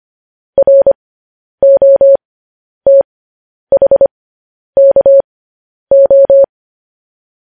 Morse code, heard as a keyed tone, about 25 wpm, spaced as for 10 wpm, 550 Hz. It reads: ROTHKO
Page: news headlines in morse code at 10 wpm